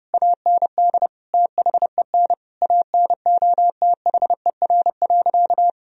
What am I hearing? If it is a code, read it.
Morse code, heard as a keyed tone, about 30 wpm, 700 Hz. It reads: AND THEN ANOTHER.
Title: Sentences from Top 1000 Words - Review Part 2 - 30wpm